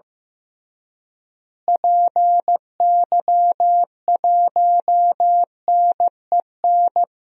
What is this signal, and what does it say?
Morse code, heard as a keyed tone, about 15 wpm, 700 Hz. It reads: PY1NEN
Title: DXCC-100 Call Signs - 6 Characters - Repeated 3x - 15wpm